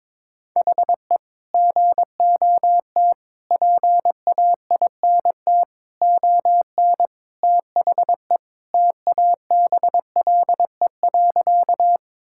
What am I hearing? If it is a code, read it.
Morse code, heard as a keyed tone, about 22 wpm, 700 Hz. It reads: HE GOT PAINT ON THE TABLE.
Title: Sentences from Top 1000 Words - Review Part 6 - 22wpm